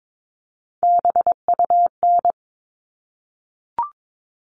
Morse code, heard as a keyed tone, about 22 wpm, 700 Hz. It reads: BUN
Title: All Cut Numbers - 3-Digit 22wpm